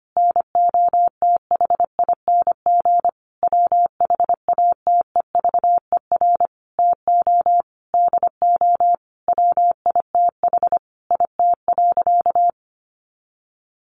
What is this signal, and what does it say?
Morse code, heard as a keyed tone, about 25 wpm, 700 Hz. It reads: NOTHING WHATEVER TO DO WITH IT.
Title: Sentences from Top 1100 Words 25wpm